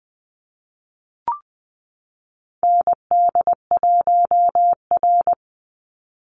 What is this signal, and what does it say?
Morse code, heard as a keyed tone, about 20 wpm, 700 Hz. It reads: ND1R
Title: Call Signs - 20wpm